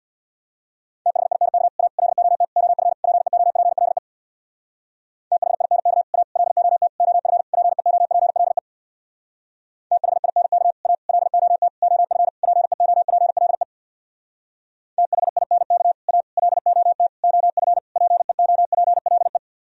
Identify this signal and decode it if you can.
Morse code, heard as a keyed tone, about 62 wpm, 700 Hz. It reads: THINK A LOT OF PEOPLE THINK A LOT OF PEOPLE THINK A LOT OF PEOPLE THINK A LOT OF PEOPLE
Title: Speed-Racing - Sets of 5 Words 50wpm